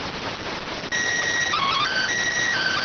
SSTV (slow-scan television), PD 290.